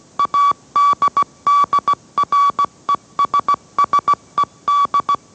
Morse code, heard as a keyed tone, about 17 wpm, 1140 Hz. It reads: ADDRESSED
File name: mrx / mrx080 18wpm